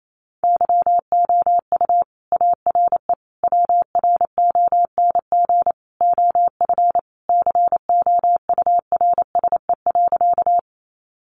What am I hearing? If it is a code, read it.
Morse code, heard as a keyed tone, about 28 wpm, 700 Hz. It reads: YOU ARE WRONG OF COURSE.
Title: Sentences from Top 900 Words 28wpm